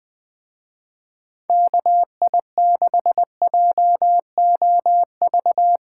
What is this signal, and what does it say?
Morse code, heard as a keyed tone, about 21 wpm, 700 Hz. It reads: KI6JOV